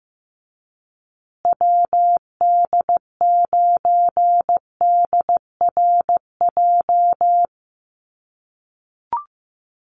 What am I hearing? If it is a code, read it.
Morse code, heard as a keyed tone, about 15 wpm, 700 Hz. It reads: WD9DRJ
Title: USA Call Signs Repeated 3x 15wpm